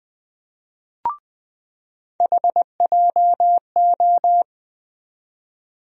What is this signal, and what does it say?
Morse code, heard as a keyed tone, about 20 wpm, 700 Hz. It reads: HJO
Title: USA Call Sign Postfixes - 3 Letters 20wpm